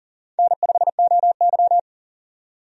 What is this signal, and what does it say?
Morse code, heard as a keyed tone, about 40 wpm, 700 Hz. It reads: N5OY
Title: USA Call Signs - 4 Characters - 40wpm